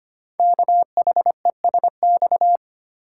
Morse code, heard as a keyed tone, about 25 wpm, 700 Hz. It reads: KHESX
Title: Code Groups - 5 Letters - 25wpm